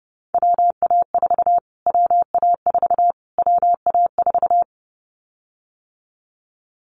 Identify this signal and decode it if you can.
Morse code, heard as a keyed tone, about 30 wpm, 690 Hz. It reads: WA4 WA4 WA4
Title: Callsign Prefixes for Top 25 DXCC - Repeated 3x - 30wpm